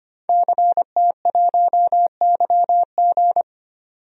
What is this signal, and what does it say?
Morse code, heard as a keyed tone, about 25 wpm, 700 Hz. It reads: CT1YG
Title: DXCC-100 Call Signs - 5 Characters - 25wpm